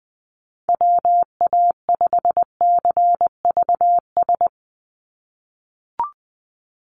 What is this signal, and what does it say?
Morse code, heard as a keyed tone, about 20 wpm, 700 Hz. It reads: WA5CVS
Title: USA Call Signs 20wpm